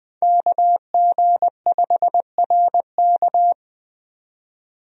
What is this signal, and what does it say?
Morse code, heard as a keyed tone, about 20 wpm, 700 Hz. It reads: KG5RK
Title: USA Call Signs - 5 Characters - 20wpm